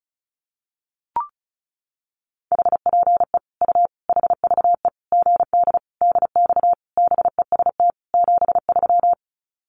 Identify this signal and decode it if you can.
Morse code, heard as a keyed tone, about 34 wpm, 700 Hz. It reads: HPE U HVE GD DX BEST 73
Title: Common QSO Phrases 35wpm